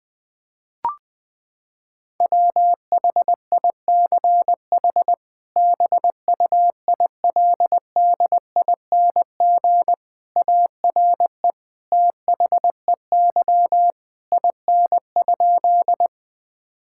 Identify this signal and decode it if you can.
Morse code, heard as a keyed tone, about 20 wpm, 700 Hz. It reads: WHICH BUILDING ARE THEY IN?
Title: Sentences from Top 1000 Words - Review Part 5 - 20wpm